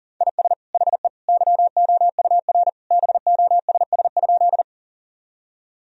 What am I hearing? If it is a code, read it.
Morse code, heard as a keyed tone, about 40 wpm, 700 Hz. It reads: IS HE YOUR BOSS?
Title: Sentences from Top 1000 Words - Review Part 4 - 40wpm